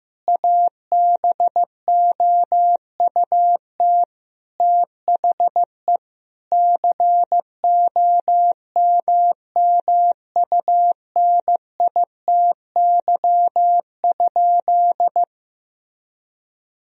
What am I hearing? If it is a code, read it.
Morse code, heard as a keyed tone, about 15 wpm, 700 Hz. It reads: ABOUT THE COMMUNITY?
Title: Sentences from Top 1000 Words - Review Part 4 - 15wpm